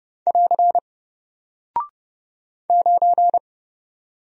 Morse code, heard as a keyed tone, about 30 wpm, 700 Hz. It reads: <AR> 9
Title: Single Character (Letter:Number:Punctuation:Prosign) Alt 30wpm